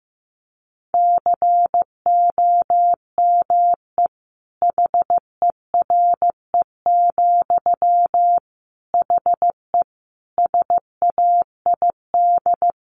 Morse code, heard as a keyed tone, about 14 wpm, 700 Hz. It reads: COME HERE, HE SAID
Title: Sentences from Top 1500 Words 15wpm